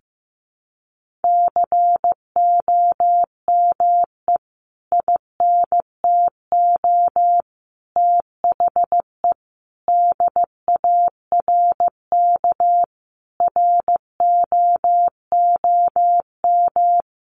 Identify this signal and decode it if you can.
Morse code, heard as a keyed tone, about 15 wpm, 700 Hz. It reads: COME INTO THE DARK ROOM
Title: Sentences from Top 1000 Words 15wpm